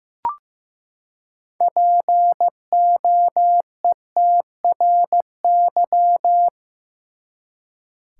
Morse code, heard as a keyed tone, about 15 wpm, 700 Hz. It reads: POETRY
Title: Six Letter Words - 15wpm